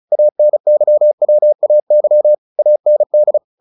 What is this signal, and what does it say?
Morse code, heard as a keyed tone, about 35 wpm, 600 Hz. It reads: ANYWAY AND
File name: the old man and the sea 2 35wpm